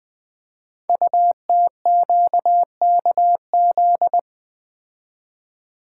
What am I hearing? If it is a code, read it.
Morse code, heard as a keyed tone, about 20 wpm, 700 Hz. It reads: UTQKZ
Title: Code Groups - 5 Letters - 20wpm